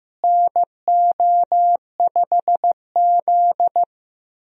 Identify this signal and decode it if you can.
Morse code, heard as a keyed tone, about 16 wpm, 700 Hz. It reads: NO5Z
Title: USA Call Signs - 4 Characters - Repeated 3x - 15wpm